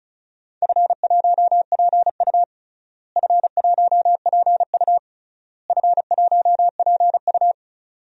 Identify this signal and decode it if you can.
Morse code, heard as a keyed tone, about 34 wpm, 700 Hz. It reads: F1PU F1PU F1PU
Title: DXCC-100 Call Signs - 4 Characters - Repeated 3x - 35wpm